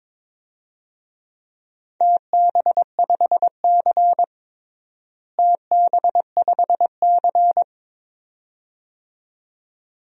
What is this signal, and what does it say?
Morse code, heard as a keyed tone, about 22 wpm, 700 Hz. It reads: TB5C TB5C
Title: DXCC-100 Call Signs - 4 Characters - Repeated 2x - 22wpm